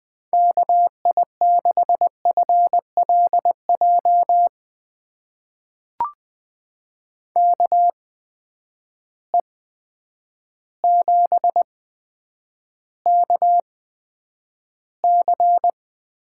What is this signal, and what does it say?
Morse code, heard as a keyed tone, about 20 wpm, spaced as for 9 wpm, 700 Hz. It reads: KI6FLJ KE7KC
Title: USA Call Signs - 6 Characters - 8x Character Spacing - 20wpm